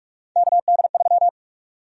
Morse code, heard as a keyed tone, about 47 wpm, 700 Hz. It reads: KD3
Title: USA Call Sign Prefixes - 45wpm